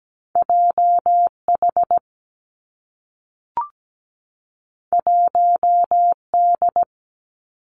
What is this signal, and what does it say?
Morse code, heard as a keyed tone, about 17 wpm, 700 Hz. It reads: JH 1D
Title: Two Challenging Letter-Number Combinations 17wpm